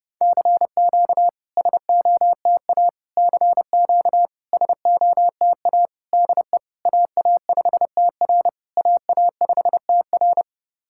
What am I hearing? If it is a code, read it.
Morse code, heard as a keyed tone, about 30 wpm, 690 Hz. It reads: CQ SOTA CQ SOTA DE AA5TR AA5TR
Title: SOTA QSO Practice 30wpm